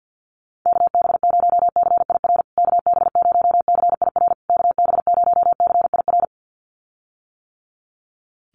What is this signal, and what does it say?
Morse code, heard as a keyed tone, about 50 wpm, 700 Hz. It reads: KB0CIR KB0CIR KB0CIR
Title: USA Call Signs Repeated 3x 50wpm